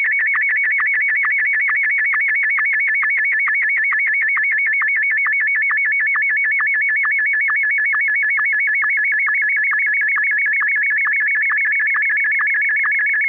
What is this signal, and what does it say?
SSTV (slow-scan television), Martin 1.